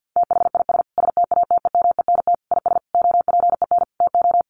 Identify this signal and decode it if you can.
Morse code, heard as a keyed tone, about 50 wpm, 700 Hz. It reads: THIS STATEMENT IS OPEN TO
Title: Sentences from Top 1300 Words 50wpm